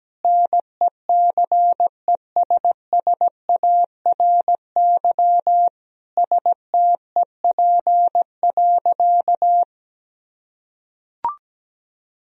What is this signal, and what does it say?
Morse code, heard as a keyed tone, about 17 wpm, 700 Hz. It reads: NECESSARY STEP.
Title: Sentences from Top 1000 Words - Review Part 2 - 17wpm